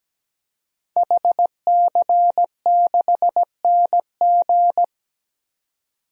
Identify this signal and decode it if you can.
Morse code, heard as a keyed tone, about 17 wpm, 700 Hz. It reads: HC6NG
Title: DXCC-100 Call Signs - 5 Characters - 17wpm